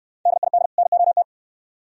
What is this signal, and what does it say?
Morse code, heard as a keyed tone, about 64 wpm, 700 Hz. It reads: DID NOT